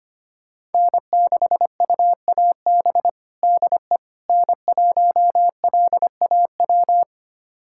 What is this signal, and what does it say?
Morse code, heard as a keyed tone, about 25 wpm, 700 Hz. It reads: N6UAB DE N1LAW
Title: USA Callsign DE Callsign - 5 Characters - 25wpm